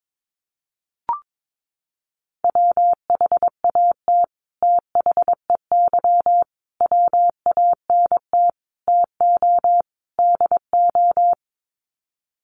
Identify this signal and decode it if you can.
Morse code, heard as a keyed tone, about 22 wpm, 700 Hz. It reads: WHAT THEY WANT TO DO
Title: Sets of 5 Words Encore2 22wpm